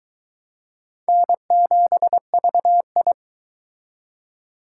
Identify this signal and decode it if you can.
Morse code, heard as a keyed tone, about 22 wpm, 700 Hz. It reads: N7VI